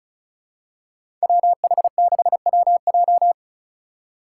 Morse code, heard as a keyed tone, about 34 wpm, 700 Hz. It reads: WH6WJ
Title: USA Call Signs - 5 Characters - 3x Character Spacing - 35wpm